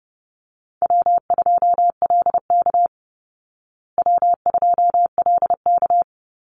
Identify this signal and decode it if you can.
Morse code, heard as a keyed tone, about 30 wpm, 700 Hz. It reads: W2LK W2LK
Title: Call Signs Repeated 2x - 30wpm